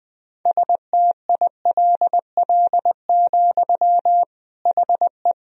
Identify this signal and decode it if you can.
Morse code, heard as a keyed tone, about 20 wpm, 700 Hz. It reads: STILL, HE
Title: Sentences from Top 1000 Words - Review Part 1 - 20wpm